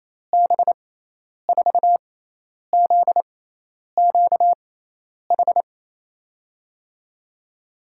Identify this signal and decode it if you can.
Morse code, heard as a keyed tone, about 28 wpm, spaced as for 10 wpm, 700 Hz. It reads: B4ZQH
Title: USA Call Signs - 6 Characters - 6x Character Spacing - 28wpm